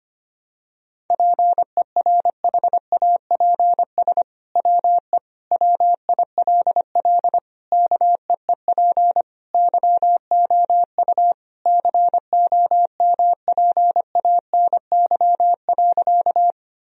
Morse code, heard as a keyed tone, about 25 wpm, 700 Hz. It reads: PERHAPS WE WILL KEEP YOU COMPANY.
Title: Sentences from Top 1000 Words - Review Part 3 - 25wpm